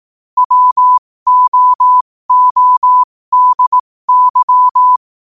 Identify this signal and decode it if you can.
Morse code, heard as a keyed tone, about 18 wpm, spaced as for 15 wpm, 1000 Hz. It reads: WOODY
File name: qotdfiles / 15wpm